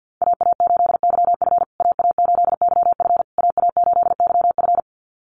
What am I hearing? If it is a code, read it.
Morse code, heard as a keyed tone, about 50 wpm, 700 Hz. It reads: AA8YF AA8YF AA8YF
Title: USA Call Signs Repeated 3x 50wpm